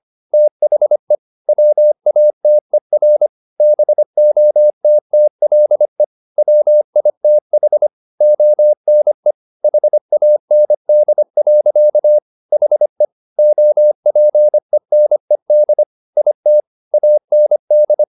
Morse code, heard as a keyed tone, about 25 wpm, 600 Hz. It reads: THE WATER BOTTLE WITH ONE HAND. HE OPENED IT AND
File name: the old man and the sea part 1 25wpm